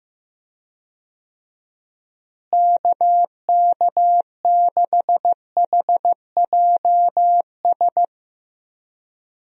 Morse code, heard as a keyed tone, about 15 wpm, 700 Hz. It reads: KK6HJS
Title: USA Call Signs - 6 Characters - Repeated 2x - 15wpm